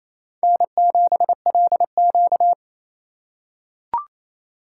Morse code, heard as a keyed tone, about 28 wpm, 700 Hz. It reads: N7LQ
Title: USA Call Signs - 4 Characters - 2x Character Spacing - 28wpm